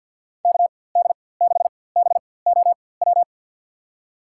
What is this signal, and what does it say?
Morse code, heard as a keyed tone, about 53 wpm, spaced as for 24 wpm, 700 Hz. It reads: KD6BOW
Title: USA Call Signs - 4x Character Spacing - 50wpm